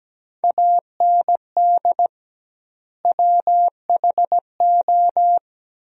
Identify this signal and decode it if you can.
Morse code, heard as a keyed tone, about 17 wpm, 700 Hz. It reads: AND WHO
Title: 2x Word-Spacing - Sentences from Top 100 Words - 17wpm